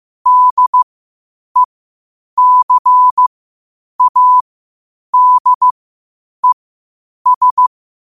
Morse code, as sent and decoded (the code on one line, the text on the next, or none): -.. -.-. .- -.. ...
DECADES